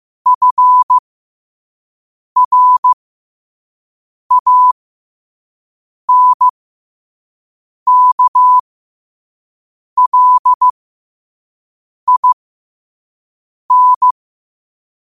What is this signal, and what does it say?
Morse code, heard as a keyed tone, about 15 wpm, spaced as for 5 wpm, 1000 Hz. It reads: FRANKLIN
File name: qotdfiles / 5wpm